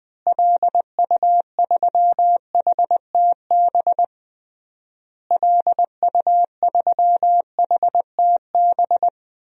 Morse code, heard as a keyed tone, about 20 wpm, 700 Hz. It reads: LU3HTB LU3HTB
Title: DXCC-100 Call Signs - 6 Characters - Repeated 3x - 20wpm